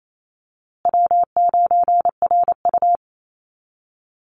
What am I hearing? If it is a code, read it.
Morse code, heard as a keyed tone, about 28 wpm, 700 Hz. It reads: W9RU
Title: USA Call Signs Repeated 2x 28wpm